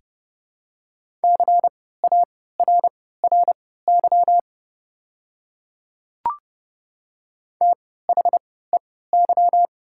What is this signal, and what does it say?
Morse code, heard as a keyed tone, about 30 wpm, spaced as for 17 wpm, 690 Hz. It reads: CARRY THEY
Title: 3x Character Spacing - Top 300 Words - 30wpm